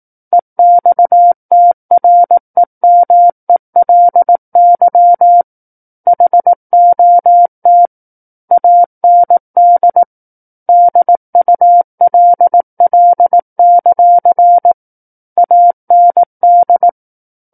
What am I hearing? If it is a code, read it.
Morse code, heard as a keyed tone, about 18 wpm, 700 Hz. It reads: EXTREMELY HOT AND DULL; AND
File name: 18 WPM